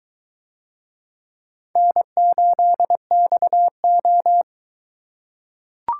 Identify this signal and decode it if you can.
Morse code, heard as a keyed tone, about 22 wpm, 700 Hz. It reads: N8XO